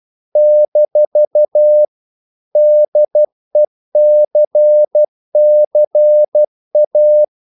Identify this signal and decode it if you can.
Morse code, heard as a keyed tone, about 12 wpm, 600 Hz. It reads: - DECCA
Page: English quotations in CW (telegraphy - morse)